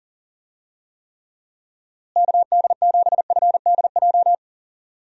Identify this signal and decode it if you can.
Morse code, heard as a keyed tone, about 40 wpm, 700 Hz. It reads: KD7FDJ
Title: USA Call Signs - 6 Characters - Repeated 3x - 40wpm